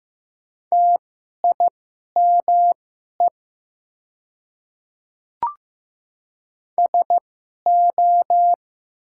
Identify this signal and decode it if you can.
Morse code, heard as a keyed tone, about 15 wpm, spaced as for 11 wpm, 700 Hz. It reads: TIME SO
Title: 2x Character Spacing - Top 100 Words - 15wpm